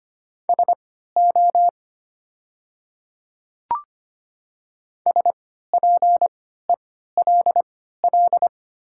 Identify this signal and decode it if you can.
Morse code, heard as a keyed tone, about 25 wpm, spaced as for 14 wpm, 700 Hz. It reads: SO SPELL
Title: 3x Character Spacing - Top 500 Words - 25wpm